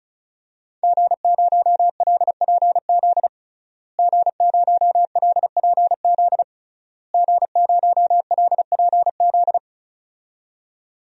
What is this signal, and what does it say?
Morse code, heard as a keyed tone, about 36 wpm, 700 Hz. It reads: G0LPZ G0LPZ G0LPZ
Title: DXCC-100 Call Signs - 5 Characters - Repeated 3x - 35wpm